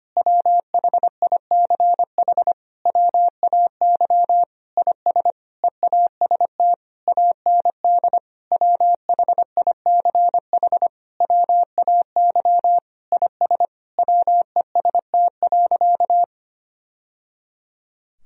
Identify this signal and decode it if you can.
Morse code, heard as a keyed tone, about 25 wpm, 700 Hz. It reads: WHICH WAY IS EAST AND WHICH WAY IS WEST.
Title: Sentences from Top 1000 Words - Review Part 3 - 25wpm